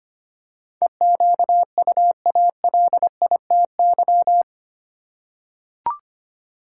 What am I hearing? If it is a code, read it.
Morse code, heard as a keyed tone, about 25 wpm, 700 Hz. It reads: EQUALITY
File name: Eight Letter Words - 25wpm